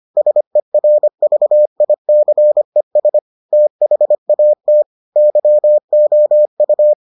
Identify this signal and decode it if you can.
Morse code, heard as a keyed tone, about 25 wpm, 600 Hz. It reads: SERVICES THAT YOU